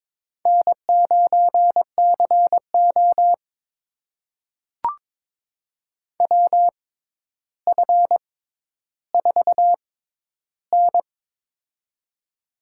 Morse code, heard as a keyed tone, about 22 wpm, spaced as for 11 wpm, 700 Hz. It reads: N9CO WF4N
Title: USA Call Signs - 4 Characters - 6x Character Spacing - 22wpm